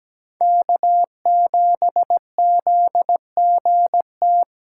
Morse code, heard as a keyed tone, about 17 wpm, 700 Hz. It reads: K7ZGT
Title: USA Call Signs - 5 Characters - 3x Character Spacing - 17wpm